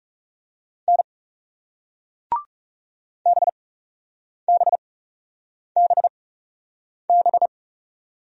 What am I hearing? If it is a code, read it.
Morse code, heard as a keyed tone, about 38 wpm, spaced as for 8 wpm, 700 Hz. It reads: N BBBB